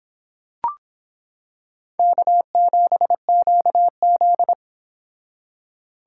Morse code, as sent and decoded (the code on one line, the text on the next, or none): -.- --... --.- --..
K7QZ